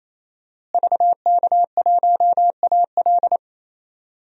Morse code, heard as a keyed tone, about 28 wpm, 700 Hz. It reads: VK1AL
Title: DXCC-100 Call Signs - 5 Characters - Repeated 3x - 28wpm